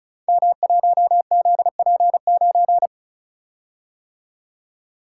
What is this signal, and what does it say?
Morse code, heard as a keyed tone, about 36 wpm, 700 Hz. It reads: M1ZP9
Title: Code Groups - 5 Letter:Numbers - 35wpm